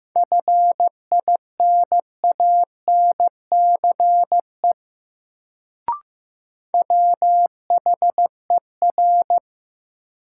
Morse code, heard as a keyed tone, about 15 wpm, 700 Hz. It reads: FINANCE WHER
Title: Sentences from Top 1500 Words - New Word Drill Alt - 15wpm